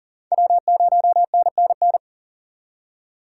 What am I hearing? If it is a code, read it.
Morse code, heard as a keyed tone, about 54 wpm, spaced as for 47 wpm, 700 Hz. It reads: W0NNN
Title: Speed Racing - USA 1x3 Call Signs 40wpm